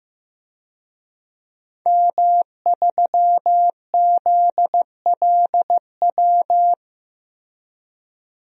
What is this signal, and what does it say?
Morse code, heard as a keyed tone, about 15 wpm, 700 Hz. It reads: M3ZLW
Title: DXCC-100 Call Signs - 5 Characters - Repeated 2x - 15wpm